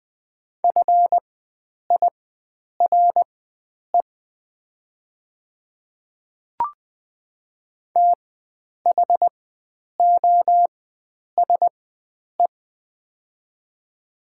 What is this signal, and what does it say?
Morse code, heard as a keyed tone, about 20 wpm, spaced as for 9 wpm, 690 Hz. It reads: FIRE THOSE
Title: 4x Character Spacing - Top 500 Words - 20wpm